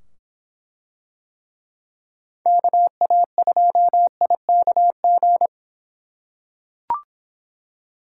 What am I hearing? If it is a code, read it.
Morse code, heard as a keyed tone, about 24 wpm, 700 Hz. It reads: KA2IKG